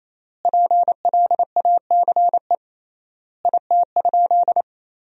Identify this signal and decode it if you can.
Morse code, heard as a keyed tone, about 28 wpm, 700 Hz. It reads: PLACE IT?
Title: 3x Word-Spacing - Sentences from Top 200 Words - 28wpm